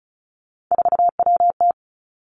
Morse code, heard as a keyed tone, about 36 wpm, 700 Hz. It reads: 4WT
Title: Three Challenging Letter-Number Combinations 35wpm